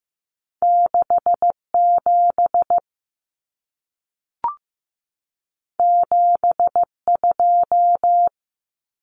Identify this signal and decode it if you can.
Morse code, heard as a keyed tone, about 15 wpm, 700 Hz. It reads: 6 7 7 2
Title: Common Cut Numbers - 2-Digit 15wpm